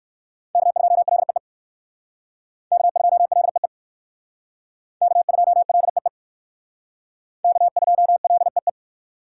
Morse code, heard as a keyed tone, about 56 wpm, 700 Hz. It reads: K2LEE K2LEE K2LEE K2LEE
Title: Speed Racing - USA 1x3 Call Signs 45wpm